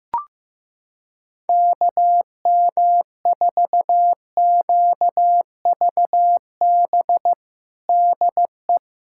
Morse code, heard as a keyed tone, about 15 wpm, 700 Hz. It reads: KM4QVB DE
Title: USA Callsign DE Callsign - 6 Characters - 15wpm